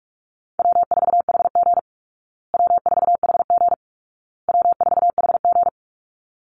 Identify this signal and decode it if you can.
Morse code, heard as a keyed tone, about 44 wpm, 700 Hz. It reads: W4HG W4HG W4HG
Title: Call Signs Repeated 2x - 45wpm